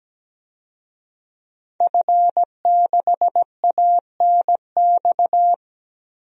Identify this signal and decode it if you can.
Morse code, heard as a keyed tone, about 17 wpm, 700 Hz. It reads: F6ANX
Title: DXCC-100 Call Signs - 5 Characters - 17wpm